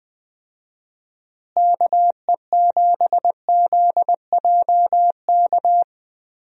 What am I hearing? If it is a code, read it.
Morse code, heard as a keyed tone, about 20 wpm, 700 Hz. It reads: KE7ZJK
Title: USA Call Signs - 6 Characters - Repeated 2x - 20wpm